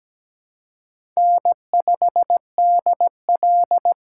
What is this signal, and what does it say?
Morse code, heard as a keyed tone, about 17 wpm, 700 Hz. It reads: N5DL
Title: USA Call Signs - 4 Characters - 3x Character Spacing - 17wpm